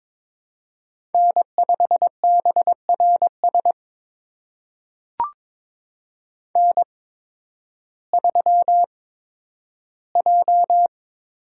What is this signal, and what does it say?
Morse code, heard as a keyed tone, about 22 wpm, spaced as for 12 wpm, 700 Hz. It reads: N5BRS N3J
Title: USA Call Signs - 5 Characters - 8x Character Spacing - 22wpm